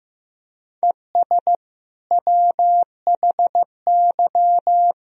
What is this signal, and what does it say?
Morse code, heard as a keyed tone, about 15 wpm, 700 Hz. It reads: ES WHY
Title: Sentences from Top 300 Words and Abbreviations 15wpm